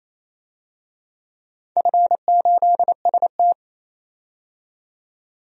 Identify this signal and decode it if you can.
Morse code, heard as a keyed tone, about 27 wpm, 690 Hz. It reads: F8ST
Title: DXCC-100 Call Signs - 4 Characters - 28wpm